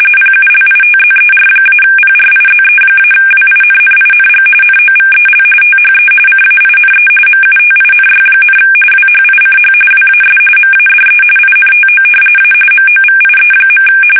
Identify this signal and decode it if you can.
RTTY, about 45.45 baud. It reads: PULITZER-PRIZE-WINNING NOVEL THE CAINE MUTINY, WHICH WAS MADE INTO A MOVIE.
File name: fsk850 / battleshipnj